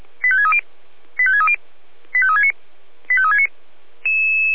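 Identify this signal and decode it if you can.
Five-tone selective call (ZVEI), heard as quick runs of five tones, tones of about 70 ms. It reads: 85630 85630 85369 85369
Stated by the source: Five-tone ZVEI selective call 85360 received on an SDR.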